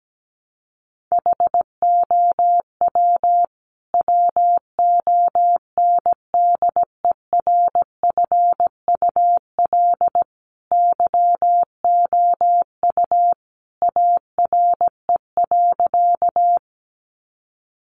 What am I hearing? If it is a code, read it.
Morse code, heard as a keyed tone, about 17 wpm, 700 Hz. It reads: HOW WONDERFUL YOU ARE.
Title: Sentences from Top 1100 Words 17wpm